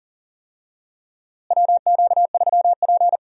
Morse code, heard as a keyed tone, about 40 wpm, 700 Hz. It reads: WQ3P
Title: USA Call Signs - 4 Characters - Repeated 3x - 40wpm